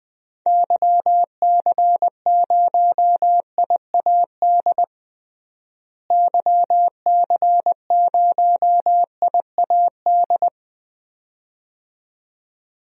Morse code, heard as a keyed tone, about 20 wpm, 700 Hz. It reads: YC0IAD YC0IAD
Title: DXCC-100 Call Signs - 6 Characters - Repeated 2x - 20wpm